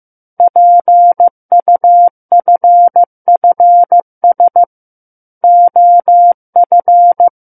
Morse code, heard as a keyed tone, about 15 wpm, 700 Hz. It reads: PUFFS OF
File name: Book One / 06-Chapter 6 - 15 WPM